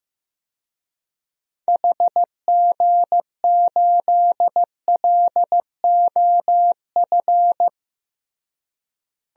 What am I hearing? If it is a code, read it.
Morse code, heard as a keyed tone, about 16 wpm, 700 Hz. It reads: HG8LOF
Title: DXCC-100 Call Signs - 6 Characters - Repeated 2x - 15wpm